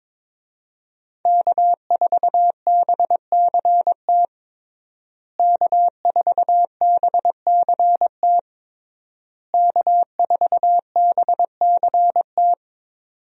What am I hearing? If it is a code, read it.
Morse code, heard as a keyed tone, about 22 wpm, 700 Hz. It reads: K4BCT K4BCT K4BCT
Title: USA Call Signs - 5 Characters - Repeated 3x - 22wpm